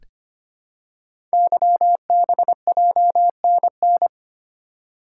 Morse code, heard as a keyed tone, about 25 wpm, 700 Hz. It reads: YBJNN
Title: Code Groups - 5 Letters - 25wpm